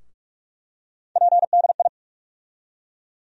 Morse code, heard as a keyed tone, about 46 wpm, 700 Hz. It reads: PDI
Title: USA Call Sign Postfixes - 3 Letters 45wpm